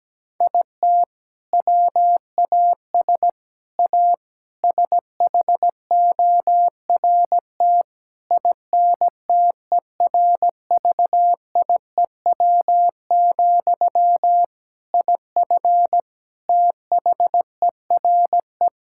Morse code, heard as a keyed tone, about 17 wpm, 700 Hz. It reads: IT WAS A SHORT INTERVIEW, IF THERE
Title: Sentences from Top 1600 Words 17wpm